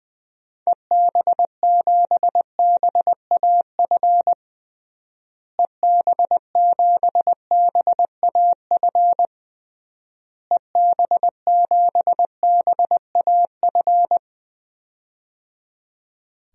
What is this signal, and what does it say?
Morse code, heard as a keyed tone, about 20 wpm, 700 Hz. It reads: EB7BAF EB7BAF EB7BAF
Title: DXCC-100 Call Signs - 6 Characters - Repeated 3x - 20wpm